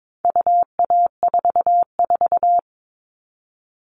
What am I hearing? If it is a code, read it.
Morse code, heard as a keyed tone, about 22 wpm, 700 Hz. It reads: UA44
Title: All Cut Numbers - 4-Digit 22wpm